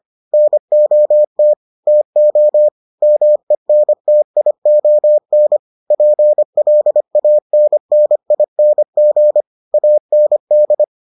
Morse code, heard as a keyed tone, about 25 wpm, 600 Hz. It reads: NOT TO MENTION PLANNING AND